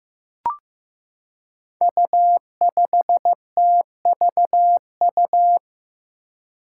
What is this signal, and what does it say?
Morse code, heard as a keyed tone, about 15 wpm, 700 Hz. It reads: U5TVU
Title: Five Challenging Letter-Number Combinations 15wpm